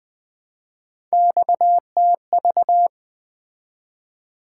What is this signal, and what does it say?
Morse code, heard as a keyed tone, about 20 wpm, 700 Hz. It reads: XTV
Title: Three Letter-Number Combinations - 20wpm